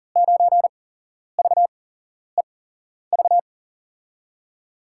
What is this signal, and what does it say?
Morse code, heard as a keyed tone, about 40 wpm, spaced as for 11 wpm, 700 Hz. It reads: 9VEV
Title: USA Call Signs - 6 Characters - 8x Character Spacing - 40wpm